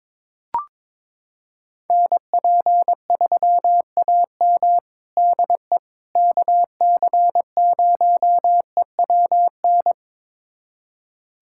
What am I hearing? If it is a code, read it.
Morse code, heard as a keyed tone, about 22 wpm, 700 Hz. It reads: NP3AM DE KC0EWN
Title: USA Callsign DE Callsign 22wpm